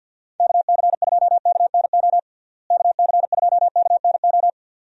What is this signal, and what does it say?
Morse code, heard as a keyed tone, about 49 wpm, 700 Hz. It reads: KC2KNO KC2KNO
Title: USA Call Signs - 6 Characters - Repeated 2x - 50wpm